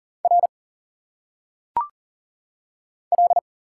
Morse code, heard as a keyed tone, about 40 wpm, 700 Hz. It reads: R L
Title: Single Letters - Phonetic Pronunciation 40wpm